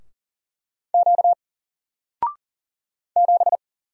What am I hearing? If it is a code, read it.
Morse code, heard as a keyed tone, about 40 wpm, 700 Hz. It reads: Q 7
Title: Single Letter-Number 40wpm